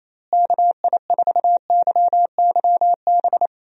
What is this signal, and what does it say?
Morse code, heard as a keyed tone, about 27 wpm, 700 Hz. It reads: KI4YYB
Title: USA Call Signs - 6 Characters - Repeated 3x - 28wpm